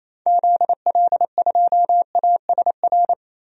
Morse code, heard as a keyed tone, about 28 wpm, 700 Hz. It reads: ZL2ASR
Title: DXCC-100 Call Signs - 6 Characters - 28wpm